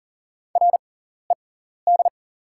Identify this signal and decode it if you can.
Morse code, heard as a keyed tone, about 40 wpm, spaced as for 14 wpm, 690 Hz. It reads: RED
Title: 6x Character Spacing - Top 500 Words - 40wpm